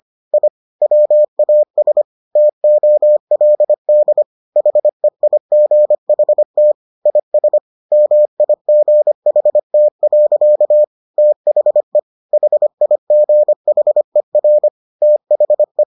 Morse code, heard as a keyed tone, about 25 wpm, 600 Hz. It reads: I WAS TOLD HEIGHT IS MIGHT. THE HIGHER THE